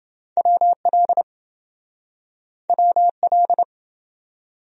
Morse code, heard as a keyed tone, about 28 wpm, 700 Hz. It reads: WL WL